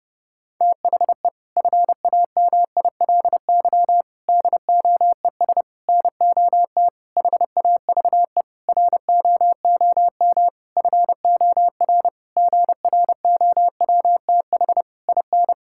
Morse code, heard as a keyed tone, about 32 wpm, 700 Hz. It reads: THE FAMILY DOES NOT HAVE ROOM FOR GROWTH IN
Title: Sentences from Top 1000 Words - Review Part 5 - 30wpm